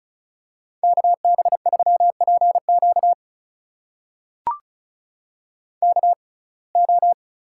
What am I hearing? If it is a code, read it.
Morse code, heard as a keyed tone, about 34 wpm, spaced as for 25 wpm, 700 Hz. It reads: KB3PQ KO
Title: USA Call Signs - 6x Character Spacing - 35wpm